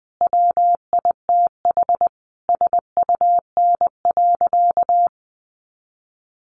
Morse code, heard as a keyed tone, about 20 wpm, 700 Hz. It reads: WITH SUN.
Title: A Sound Of Thunder 20wpm